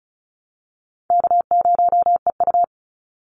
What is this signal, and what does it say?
Morse code, heard as a keyed tone, about 34 wpm, 700 Hz. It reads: K0EU
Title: Call Signs Repeated 2x - 35wpm